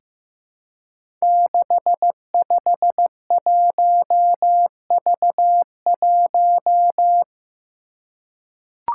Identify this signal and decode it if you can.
Morse code, heard as a keyed tone, about 15 wpm, 700 Hz. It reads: 651V1
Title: Five Challenging Letter-Number Combinations 15wpm